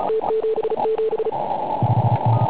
Morse code, heard as a keyed tone, about 37 wpm, 440 Hz, with noise. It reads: T77
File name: Pile UP CUC204